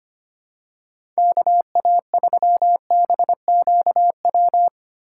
Morse code, heard as a keyed tone, about 25 wpm, 700 Hz. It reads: KA3BQW
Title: USA Call Signs - 6 Characters - 4x Character Spacing - 25wpm